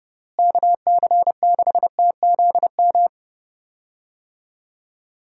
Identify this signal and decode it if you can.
Morse code, heard as a keyed tone, about 29 wpm, 700 Hz. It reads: KC6TZM
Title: USA Call Signs - 6 Characters - Repeated 2x - 30wpm